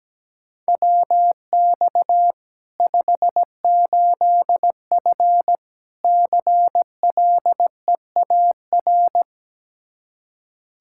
Morse code, heard as a keyed tone, about 17 wpm, 700 Hz. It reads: WX 58F CLEAR
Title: Practice QSO - Exchange 2 - Info - 17wpm